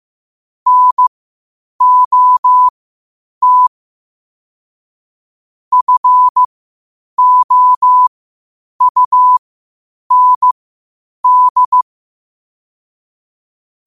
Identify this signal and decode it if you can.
Morse code, heard as a keyed tone, about 15 wpm, spaced as for 8 wpm, 1000 Hz. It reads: NOT FOUND
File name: qotdfiles / 8wpm